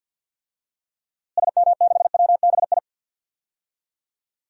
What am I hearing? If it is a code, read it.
Morse code, heard as a keyed tone, about 51 wpm, 700 Hz. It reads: SM6WBI
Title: DXCC-100 Call Signs - 6 Characters - Repeated 2x - 50wpm